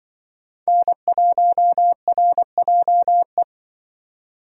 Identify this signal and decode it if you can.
Morse code, heard as a keyed tone, about 23 wpm, 700 Hz. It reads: N1RJE
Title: Speed Racing - USA 1x3 Call Signs 20wpm